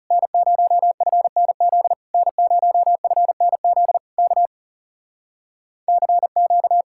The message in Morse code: N0FNZ N0FNZ K CQ

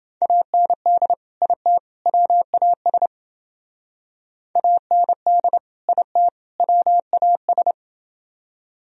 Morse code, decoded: AND IT WAS AND IT WAS

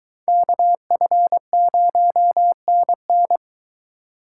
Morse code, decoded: KF0NN

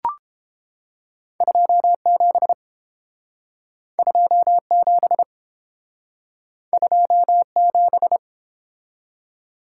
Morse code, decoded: 27 27 27

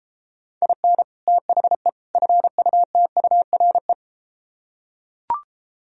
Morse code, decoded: IN THE FUTURE

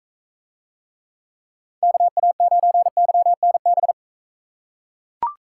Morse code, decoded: KA9YNB

